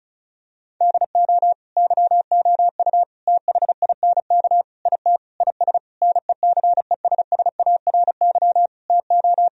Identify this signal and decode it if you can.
Morse code, heard as a keyed tone, about 34 wpm, 700 Hz. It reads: DO YOU THINK IT IS NECESSARY TO
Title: Sentences from Top 1000 Words - Review Part 1 - 35wpm